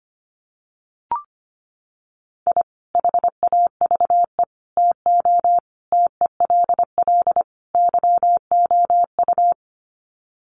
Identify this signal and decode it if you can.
Morse code, heard as a keyed tone, about 25 wpm, 700 Hz. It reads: I HAVE TO TELL YOU
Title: Sets of 5 Words 25wpm